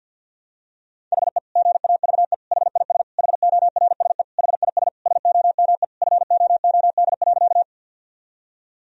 Morse code, heard as a keyed tone, about 51 wpm, 700 Hz. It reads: HE GAVE HIS HORSE HIS SOME FOOD.
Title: Sentences from Top 1000 Words - Review Part 1 - 50wpm